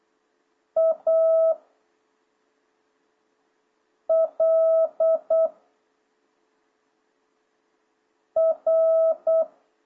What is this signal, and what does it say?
Morse code, heard as a keyed tone, about 8 wpm, 650 Hz. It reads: A L R